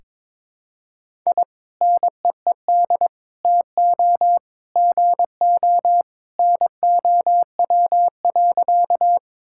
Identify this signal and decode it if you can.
Morse code, heard as a keyed tone, about 22 wpm, 700 Hz. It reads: I NEED TO GO NOW.
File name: practice / 22wpm / Sentences with 2-Word Phrases - 22wpm